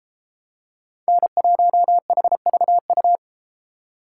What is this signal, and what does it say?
Morse code, heard as a keyed tone, about 31 wpm, 700 Hz. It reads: N1HVU